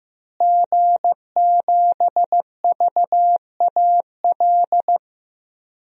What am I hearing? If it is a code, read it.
Morse code, heard as a keyed tone, about 15 wpm, 700 Hz. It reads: G7VAL
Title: DXCC-100 Call Signs - 5 Characters - Repeated 2x - 15wpm